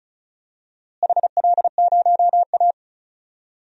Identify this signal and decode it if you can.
Morse code, heard as a keyed tone, about 36 wpm, 700 Hz. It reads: HL0A
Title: DXCC-100 Call Signs - 4 Characters - Repeated 2x - 35wpm